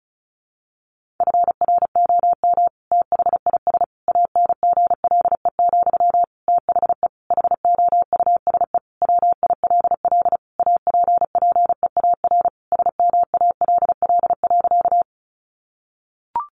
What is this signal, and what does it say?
Morse code, heard as a keyed tone, about 34 wpm, 700 Hz. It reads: FROM THIS ANGLE, THE HOUSE WILL APPEAR SMALL.
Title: Sentences from Top 1500 Words 35wpm